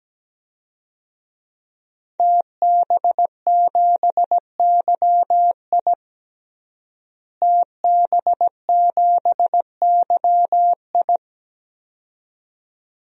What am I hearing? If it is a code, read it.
Morse code, heard as a keyed tone, about 17 wpm, 700 Hz. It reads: TB7YI TB7YI
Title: DXCC-100 Call Signs - 5 Characters - Repeated 2x - 17wpm